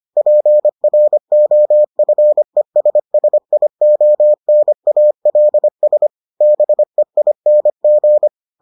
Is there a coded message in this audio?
Yes: Morse code from 0.2 to 8.3 s, heard as a keyed tone, about 25 wpm, 600 Hz. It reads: PROFESSIONALS BEING